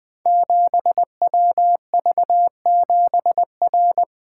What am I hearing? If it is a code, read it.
Morse code, heard as a keyed tone, about 20 wpm, 700 Hz. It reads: 7WV7R
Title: Code Groups - 5 Letter:Numbers - 20wpm